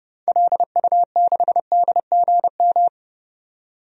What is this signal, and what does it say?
Morse code, heard as a keyed tone, about 29 wpm, 700 Hz. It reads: LU6DGM
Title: DXCC-100 Call Signs - 6 Characters - 30wpm